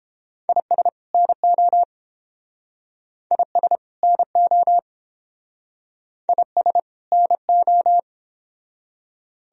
Morse code, heard as a keyed tone, about 29 wpm, 700 Hz. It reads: IS NO IS NO IS NO